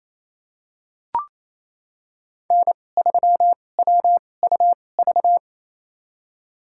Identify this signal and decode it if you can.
Morse code, heard as a keyed tone, about 28 wpm, spaced as for 20 wpm, 700 Hz. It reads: N3WUV
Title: USA Call Signs - 5 Characters - 2x Character Spacing - 28wpm